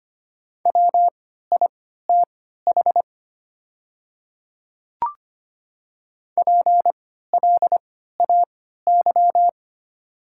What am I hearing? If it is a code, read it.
Morse code, heard as a keyed tone, about 25 wpm, spaced as for 14 wpm, 700 Hz. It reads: WITH PLAY
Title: 3x Character Spacing - Top 300 Words - 25wpm